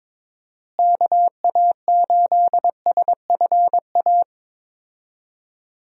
Morse code, heard as a keyed tone, about 22 wpm, 700 Hz. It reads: KA8SFA
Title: USA Call Signs - 6 Characters - Repeated 2x - 22wpm